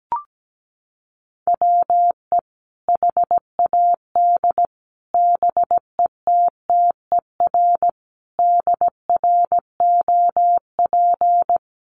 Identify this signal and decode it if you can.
Morse code, heard as a keyed tone, about 17 wpm, 700 Hz. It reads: WE HAD BETTER DROP
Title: Sentences from Top 1400 Words 17wpm